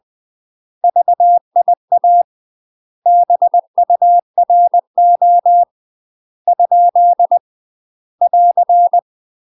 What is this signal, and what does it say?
Morse code, heard as a keyed tone, about 20 wpm, 700 Hz. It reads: VIA BURO ? <AR>
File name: UK MASTER QSOs TRAINING / 20WPM / UK MASTER QSOs 0071